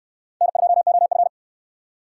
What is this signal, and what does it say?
Morse code, heard as a keyed tone, about 68 wpm, 700 Hz. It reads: N2OF